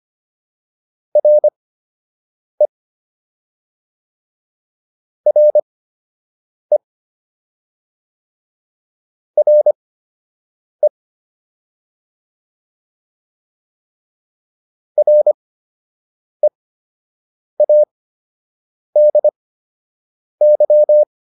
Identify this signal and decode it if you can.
Morse code, heard as a keyed tone, about 25 wpm, spaced as for 7 wpm, 600 Hz. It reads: RE RE RE READY